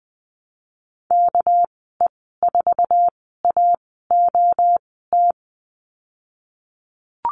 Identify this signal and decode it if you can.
Morse code, heard as a keyed tone, about 20 wpm, spaced as for 15 wpm, 700 Hz. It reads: KE4AOT
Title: 200 Lesson 32g - Letter k - Call Signs 20wpm